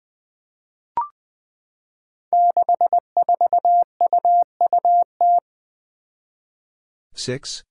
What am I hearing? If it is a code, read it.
Morse code, heard as a keyed tone, about 20 wpm, 700 Hz. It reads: 64UUT
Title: Five Challenging Letter-Number Combinations 20wpm